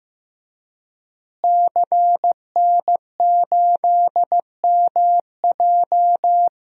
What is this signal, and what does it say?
Morse code, heard as a keyed tone, about 15 wpm, 700 Hz. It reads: CN8MJ
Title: DXCC-100 Call Signs - 5 Characters - Repeated 2x - 15wpm